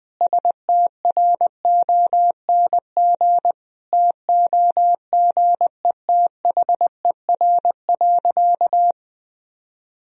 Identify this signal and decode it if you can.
Morse code, heard as a keyed tone, about 20 wpm, 700 Hz. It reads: STRONG TOGETHER.
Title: Sentences from Top 500 Words and Abbreviations 20wpm